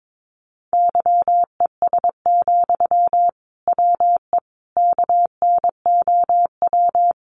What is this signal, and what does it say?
Morse code, heard as a keyed tone, about 22 wpm, 700 Hz. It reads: YES, WE KNOW
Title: Sentences from Top 500 Words 22wpm